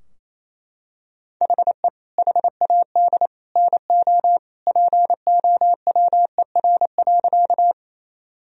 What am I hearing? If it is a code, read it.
Morse code, heard as a keyed tone, about 28 wpm, 700 Hz. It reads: HE HAD NO POWER.
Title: Sentences from Top 1000 Words - Review Part 2 - 28wpm